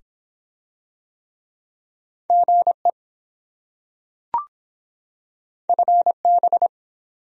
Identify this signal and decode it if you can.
Morse code, heard as a keyed tone, about 24 wpm, 700 Hz. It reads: GE FB